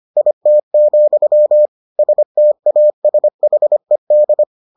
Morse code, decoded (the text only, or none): IT, STASHED